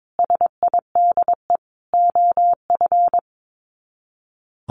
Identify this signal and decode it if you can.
Morse code, heard as a keyed tone, about 22 wpm, 700 Hz. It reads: SIDE OF
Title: Sets of 5 Words 22wpm